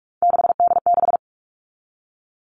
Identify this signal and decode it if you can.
Morse code, heard as a keyed tone, about 44 wpm, 700 Hz. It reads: 6D6
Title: All Cut Numbers - 3-Digit 45wpm